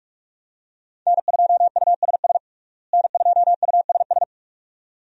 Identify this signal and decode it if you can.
Morse code, heard as a keyed tone, about 41 wpm, 700 Hz. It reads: N2USS N2USS
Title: USA Call Signs - 5 Characters - Repeated 2x - 45wpm